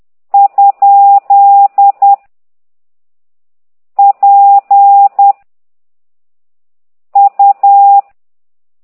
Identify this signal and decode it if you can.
Morse code, heard as a keyed tone, about 10 wpm, 800 Hz. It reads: ? P U
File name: ct2idw / exerc8